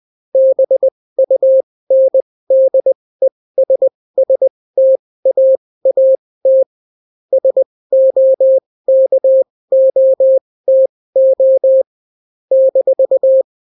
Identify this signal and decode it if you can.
Morse code, heard as a keyed tone, about 20 wpm, spaced as for 16 wpm, 530 Hz. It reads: BUNDESSTAAT SOKOTO -